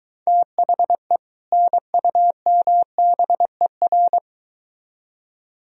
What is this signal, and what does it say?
Morse code, heard as a keyed tone, about 23 wpm, 700 Hz. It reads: THE NUMBER